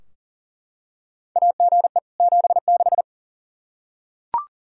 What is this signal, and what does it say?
Morse code, heard as a keyed tone, about 40 wpm, 700 Hz. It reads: AGE 76
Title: Practice QSO - Exchange 3 - More Info - 40wpm